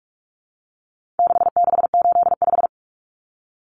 Morse code, heard as a keyed tone, about 44 wpm, 700 Hz. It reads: 6685
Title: Common Cut Numbers - 4-Digit 45wpm